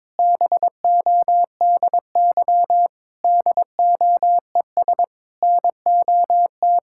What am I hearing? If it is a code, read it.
Morse code, heard as a keyed tone, about 22 wpm, 700 Hz. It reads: BODY DOES NOT
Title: Sentences from Top 1000 Words - Review Part 6 - 22wpm